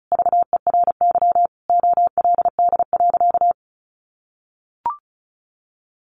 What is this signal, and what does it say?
Morse code, heard as a keyed tone, about 34 wpm, 700 Hz. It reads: VERY OLD.
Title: Sentences from Top 1100 Words 35wpm